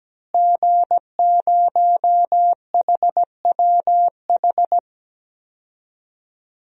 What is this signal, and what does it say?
Morse code, heard as a keyed tone, about 17 wpm, 700 Hz. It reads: G0HWH
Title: DXCC-100 Call Signs - 5 Characters - Repeated 3x - 17wpm